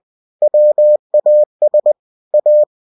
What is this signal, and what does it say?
Morse code, heard as a keyed tone, about 20 wpm, 600 Hz. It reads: WAS A
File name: cw-books / winnie-the-pooh / winnie-20wpm-page-0040